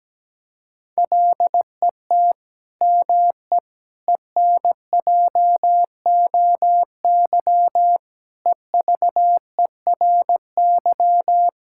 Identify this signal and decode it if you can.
Morse code, heard as a keyed tone, about 17 wpm, 700 Hz. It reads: LET ME ENJOY EVERY
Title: Sentences from Top 1000 Words - Review Part 5 - 17wpm